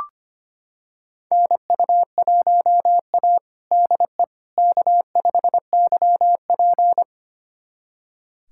Morse code, heard as a keyed tone, about 25 wpm, 700 Hz. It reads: NU1A DE K5YP
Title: USA Callsign DE Callsign - 4 Characters - 25wpm